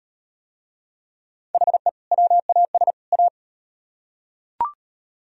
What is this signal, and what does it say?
Morse code, heard as a keyed tone, about 37 wpm, 700 Hz. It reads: HE WAS A